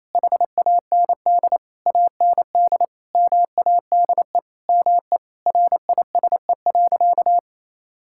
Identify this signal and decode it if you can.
Morse code, heard as a keyed tone, about 28 wpm, 700 Hz. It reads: HAND AND MADE ME RISE.
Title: Sentences from Top 1000 Words - Review Part 6 - 28wpm